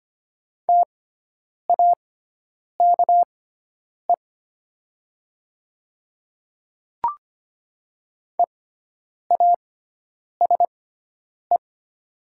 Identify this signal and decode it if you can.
Morse code, heard as a keyed tone, about 25 wpm, spaced as for 9 wpm, 700 Hz. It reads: TAKE EASE
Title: 6x Character Spacing - Top 500 Words - 25wpm